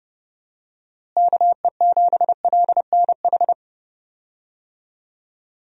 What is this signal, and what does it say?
Morse code, heard as a keyed tone, about 30 wpm, 700 Hz. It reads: KE7LNH
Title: USA Call Signs - 6 Characters - Repeated 3x - 30wpm